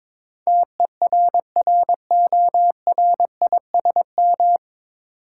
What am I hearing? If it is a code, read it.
Morse code, heard as a keyed tone, about 22 wpm, 700 Hz. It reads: TERRORISM
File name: 9 Letter Words - First 3 Letters Given - 22wpm